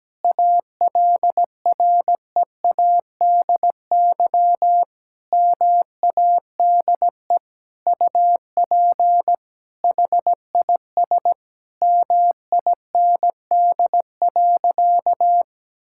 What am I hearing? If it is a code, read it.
Morse code, heard as a keyed tone, about 17 wpm, 700 Hz. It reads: ALREADY MADE UP HIS MIND.
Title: Sentences from Top 1000 Words - Review Part 5 - 17wpm